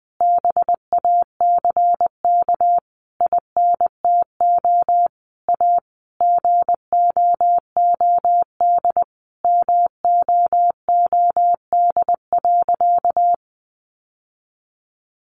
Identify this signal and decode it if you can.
Morse code, heard as a keyed tone, about 20 wpm, 700 Hz. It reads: BACK INTO A GOOD MOOD.
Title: Sentences from Top 1100 Words 20wpm